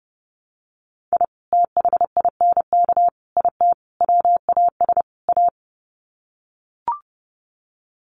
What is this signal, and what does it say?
Morse code, heard as a keyed tone, about 30 wpm, 700 Hz. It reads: I THINK IT WAS A
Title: Sets of 5 Words 30wpm